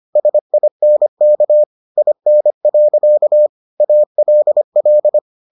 Morse code, heard as a keyed tone, about 25 wpm, 600 Hz. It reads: SINK IN. ALL